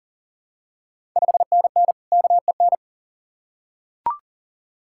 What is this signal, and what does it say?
Morse code, heard as a keyed tone, about 40 wpm, 700 Hz. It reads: 5NN KEN
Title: Ontario QSO Party Exchange 40wpm